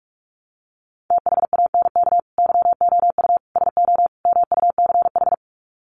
Morse code, heard as a keyed tone, about 44 wpm, 700 Hz. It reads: THANK YOU SO MUCH
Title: Sets of 4 Words 45wpm